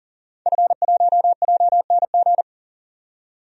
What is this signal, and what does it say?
Morse code, heard as a keyed tone, about 37 wpm, 700 Hz. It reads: F1JNG
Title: DXCC-100 Call Signs - 5 Characters - 40wpm